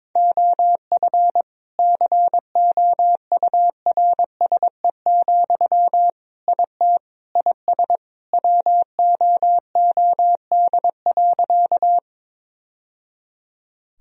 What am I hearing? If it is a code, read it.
Morse code, heard as a keyed tone, about 22 wpm, 700 Hz. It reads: OF COURSE, IT IS WOOD.
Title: Sentences from Top 1000 Words - Review Part 2 - 22wpm